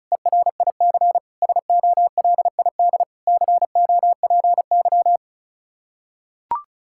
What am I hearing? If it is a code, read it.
Morse code, heard as a keyed tone, about 34 wpm, 700 Hz. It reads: ERIC SOLID COPY
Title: Practice QSO - Exchange 3 - More Info - 35wpm